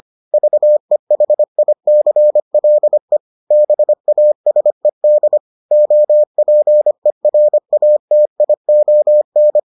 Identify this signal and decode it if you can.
Morse code, heard as a keyed tone, about 25 wpm, 600 Hz. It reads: VEHICLE BASED OPERATION